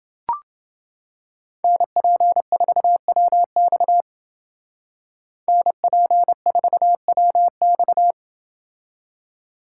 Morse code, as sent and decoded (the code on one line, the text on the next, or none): -. .--. ....- .-- -..- / -. .--. ....- .-- -..-
NP4WX NP4WX